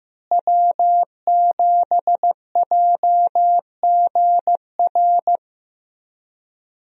Morse code, heard as a keyed tone, about 15 wpm, 700 Hz. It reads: W7JGR
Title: USA Call Signs - 5 Characters - Repeated 3x - 15wpm